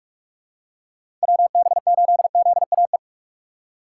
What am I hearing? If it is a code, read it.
Morse code, heard as a keyed tone, about 42 wpm, 700 Hz. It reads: WB8ZAE
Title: USA Call Signs - 6 Characters - 2x Character Spacing - 45wpm